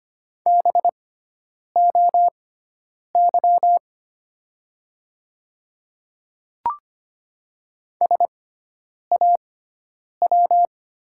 Morse code, heard as a keyed tone, about 25 wpm, spaced as for 9 wpm, 700 Hz. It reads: BOY SAW
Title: 6x Character Spacing - Top 300 Words - 25wpm